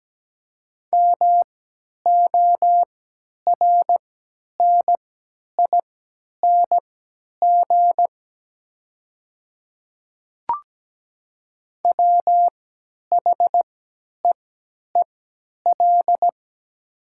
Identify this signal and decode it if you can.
Morse code, heard as a keyed tone, about 17 wpm, spaced as for 10 wpm, 700 Hz. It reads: MORNING WHEEL
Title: 3x Character Spacing - Top 500 Words - 17wpm